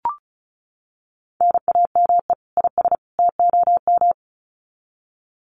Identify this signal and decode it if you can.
Morse code, heard as a keyed tone, about 34 wpm, 700 Hz. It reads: NAME IS TOM
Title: Common QSO Phrases 35wpm